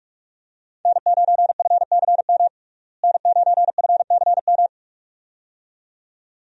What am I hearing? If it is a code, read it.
Morse code, heard as a keyed tone, about 46 wpm, 700 Hz. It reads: N9FCM N9FCM
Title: USA Call Signs - 5 Characters - Repeated 2x - 45wpm